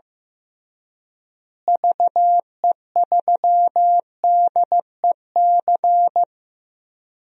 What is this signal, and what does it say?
Morse code, heard as a keyed tone, about 15 wpm, 700 Hz. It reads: VE3DEC
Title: DXCC-100 Call Signs - 6 Characters - Repeated 3x - 15wpm